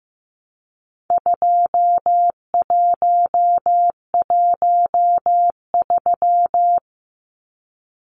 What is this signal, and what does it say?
Morse code, heard as a keyed tone, about 15 wpm, 700 Hz. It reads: 2113
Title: Numbers - 4-Digit 15wpm